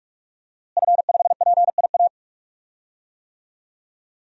Morse code, heard as a keyed tone, about 46 wpm, 700 Hz. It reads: F5PIA
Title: DXCC-100 Call Signs - 5 Characters - Repeated 3x - 45wpm